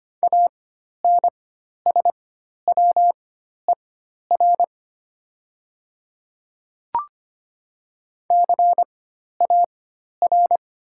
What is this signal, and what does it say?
Morse code, heard as a keyed tone, about 25 wpm, spaced as for 12 wpm, 700 Hz. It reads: ANSWER CAR
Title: 4x Character Spacing - Top 300 Words - 25wpm